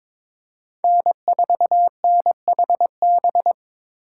Morse code, heard as a keyed tone, about 22 wpm, 700 Hz. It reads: N4NHB
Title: USA Call Signs - 5 Characters - 8x Character Spacing - 22wpm